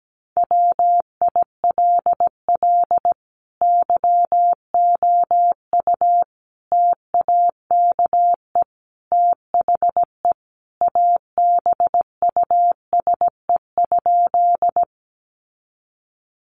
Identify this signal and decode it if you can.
Morse code, heard as a keyed tone, about 17 wpm, 700 Hz. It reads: WILL YOU TAKE THE ABUSE?
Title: Sentences from Top 1300 Words 17wpm